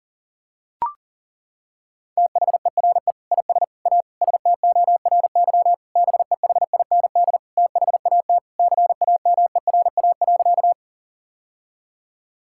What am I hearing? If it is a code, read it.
Morse code, heard as a keyed tone, about 40 wpm, 700 Hz. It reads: THERE IS A STORY BEHIND THAT CAMERA.
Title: Sentences from Top 1000 Words - Review Part 5 - 40wpm